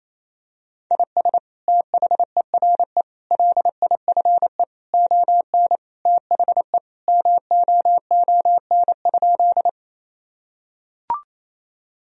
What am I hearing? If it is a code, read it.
Morse code, heard as a keyed tone, about 28 wpm, 700 Hz. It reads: IS THERE LIFE ON THE MOON?
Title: Sentences from Top 1000 Words - Review Part 3 - 28wpm